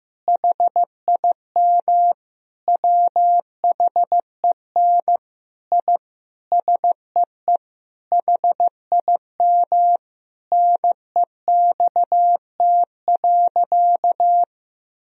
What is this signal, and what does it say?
Morse code, heard as a keyed tone, about 15 wpm, 700 Hz. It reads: HIM WHEN I SEE HIM NEXT.
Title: Sentences from Top 1000 Words - Review Part 2 - 15wpm